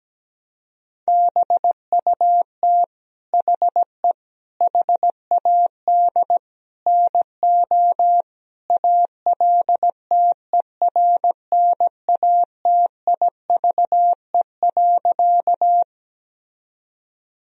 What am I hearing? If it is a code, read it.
Morse code, heard as a keyed tone, about 17 wpm, 700 Hz. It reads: BUT HE HAD NO ALTERNATIVE.
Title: Sentences from Top 1000 Words - Review Part 4 - 17wpm